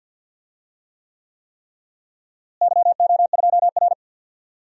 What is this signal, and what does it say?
Morse code, heard as a keyed tone, about 51 wpm, 700 Hz. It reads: YO2R
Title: DXCC-100 Call Signs - 4 Characters - Repeated 3x - 50wpm